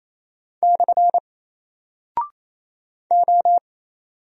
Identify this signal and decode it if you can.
Morse code, heard as a keyed tone, about 27 wpm, 700 Hz. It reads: / O
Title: Single Character (Letter:Number:Punctuation:Prosign) 28wpm